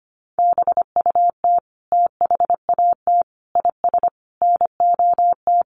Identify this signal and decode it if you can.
Morse code, heard as a keyed tone, about 25 wpm, 700 Hz. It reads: BUT THAT IS NOT
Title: Sentences from Top 600 Words 25wpm